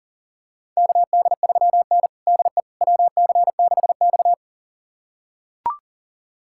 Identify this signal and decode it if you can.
Morse code, heard as a keyed tone, about 40 wpm, 700 Hz. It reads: KD3N DE WC6X
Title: USA Callsign DE Callsign - 4 Characters - 40wpm